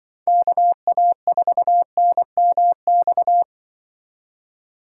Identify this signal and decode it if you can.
Morse code, heard as a keyed tone, about 24 wpm, 700 Hz. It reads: KA4NMX
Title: Speed Racing - USA Call Signs 20wpm